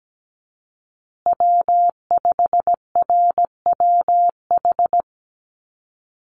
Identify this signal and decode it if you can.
Morse code, heard as a keyed tone, about 17 wpm, 700 Hz. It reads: W5RWH
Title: USA Call Signs Repeated 2x 17wpm